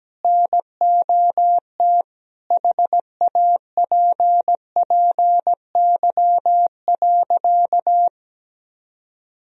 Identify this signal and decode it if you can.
Morse code, heard as a keyed tone, about 17 wpm, 700 Hz. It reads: NOT HAPPY.
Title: Sentences from Top 1000 Words - Review Part 3 - 17wpm